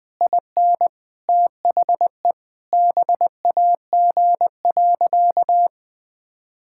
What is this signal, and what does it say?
Morse code, heard as a keyed tone, about 21 wpm, 700 Hz. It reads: IN THE BAG.
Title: Sentences with 3-Word Phrases - 20wpm